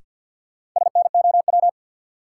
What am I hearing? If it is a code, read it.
Morse code, heard as a keyed tone, about 49 wpm, 700 Hz. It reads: SNOW
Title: Sentences from Top 500 Words - New Word Drill - 50wpm